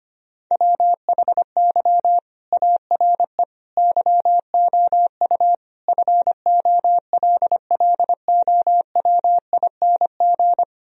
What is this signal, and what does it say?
Morse code, heard as a keyed tone, about 25 wpm, 700 Hz. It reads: WHY ARE YOU FOLLOWING
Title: Sentences from Top 1000 Words - Review Part 4 - 25wpm